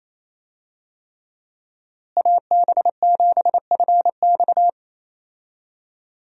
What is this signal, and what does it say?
Morse code, heard as a keyed tone, about 28 wpm, 700 Hz. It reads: AB7FX
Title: USA Call Signs - 5 Characters - Repeated 3x - 28wpm